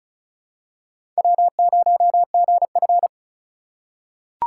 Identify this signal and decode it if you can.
Morse code, heard as a keyed tone, about 36 wpm, 700 Hz. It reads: W0GF
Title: USA Call Signs - 4 Characters - 4x Character Spacing - 35wpm